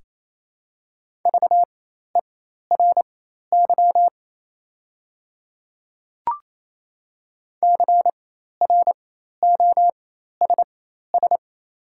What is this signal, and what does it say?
Morse code, heard as a keyed tone, about 27 wpm, spaced as for 13 wpm, 690 Hz. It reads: VERY CROSS
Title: 4x Character Spacing - Top 500 Words - 28wpm